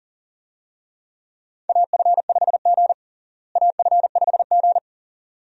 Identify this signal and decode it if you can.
Morse code, heard as a keyed tone, about 40 wpm, 700 Hz. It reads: AF5G AF5G
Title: USA Call Signs - 4 Characters - Repeated 2x - 40wpm